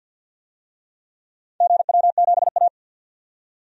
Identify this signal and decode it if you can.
Morse code, heard as a keyed tone, about 51 wpm, 700 Hz. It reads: GW7A
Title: DXCC-100 Call Signs - 4 Characters - 50wpm